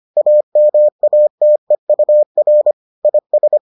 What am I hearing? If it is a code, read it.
Morse code, heard as a keyed tone, about 25 wpm, 600 Hz. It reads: AMATEUR IS